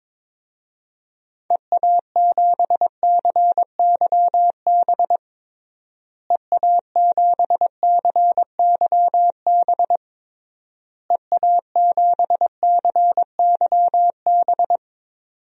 Morse code, heard as a keyed tone, about 22 wpm, 700 Hz. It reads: EA7CYB EA7CYB EA7CYB
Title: DXCC-100 Call Signs - 6 Characters - Repeated 3x - 22wpm